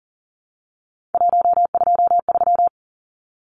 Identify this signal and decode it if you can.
Morse code, heard as a keyed tone, about 40 wpm, 700 Hz. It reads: 123
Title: Numbers - 3-Digit 40wpm